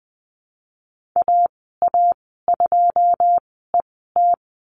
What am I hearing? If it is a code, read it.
Morse code, heard as a keyed tone, about 20 wpm, spaced as for 15 wpm, 700 Hz. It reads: AA2ET
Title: 088 Lesson 15g - Number 5 - Call Signs 20wpm